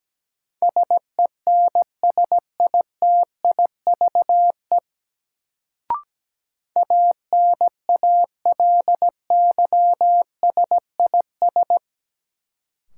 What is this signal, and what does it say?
Morse code, heard as a keyed tone, about 17 wpm, 700 Hz. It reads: SENSITIVE ANALYSIS
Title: Sentences from Top 1400 Words - New Word Drill Alt - 17wpm